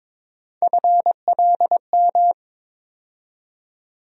Morse code, heard as a keyed tone, about 22 wpm, 700 Hz. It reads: FLM
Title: USA Call Sign Postfixes - 3 Letters 22wpm